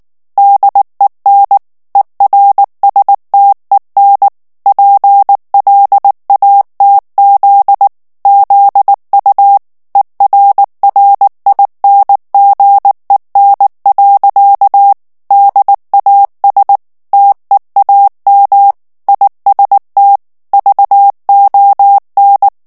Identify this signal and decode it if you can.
Morse code, heard as a keyed tone, about 19 wpm, 800 Hz. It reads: DEN ERSTEN PLATZ ZU ERRINGEN. DAS TEAM IST VON
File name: morse mp3 / ALLEMAND texte clair 2 vitesse 18 mots minute